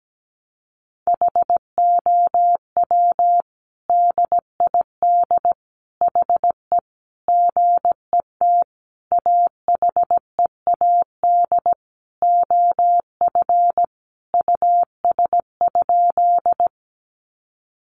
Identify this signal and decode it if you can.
Morse code, heard as a keyed tone, about 17 wpm, 700 Hz. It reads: HOW DID HE GET AHEAD OF US?
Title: Sentences from Top 1100 Words 17wpm